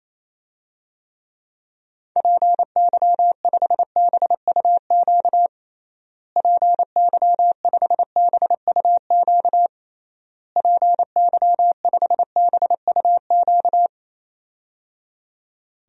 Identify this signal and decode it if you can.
Morse code, heard as a keyed tone, about 28 wpm, 700 Hz. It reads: PY5BUQ PY5BUQ PY5BUQ
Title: DXCC-100 Call Signs - 6 Characters - Repeated 3x - 28wpm